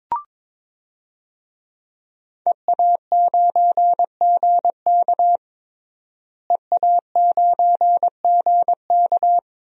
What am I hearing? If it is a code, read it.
Morse code, heard as a keyed tone, about 22 wpm, 700 Hz. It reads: EA9GK EA9GK
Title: DXCC-100 Call Signs - 5 Characters - Repeated 2x - 22wpm